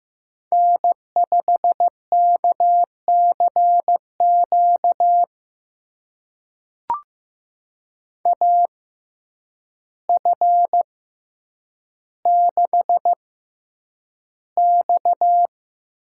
Morse code, heard as a keyed tone, about 15 wpm, spaced as for 8 wpm, 700 Hz. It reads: N5KCQ AF6X
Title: USA Call Signs - 5 Characters - 6x Character Spacing - 15wpm